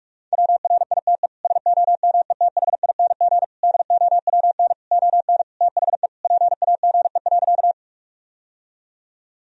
Morse code, heard as a keyed tone, about 46 wpm, 700 Hz. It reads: WRITE SOMETHING DOWN ON THE PAGE.
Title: Sentences from Top 1000 Words - Review Part 1 - 45wpm